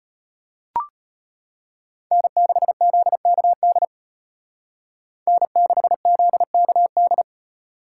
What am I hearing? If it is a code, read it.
Morse code, heard as a keyed tone, about 33 wpm, 700 Hz. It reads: N6ZKD N6ZKD